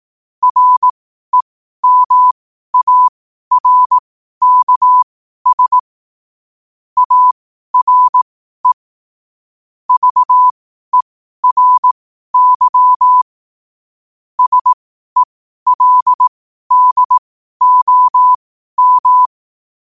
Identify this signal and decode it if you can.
Morse code, heard as a keyed tone, about 18 wpm, spaced as for 12 wpm, 1000 Hz. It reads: REMARKS ARE VERY SELDOM
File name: qotdfiles / 12wpm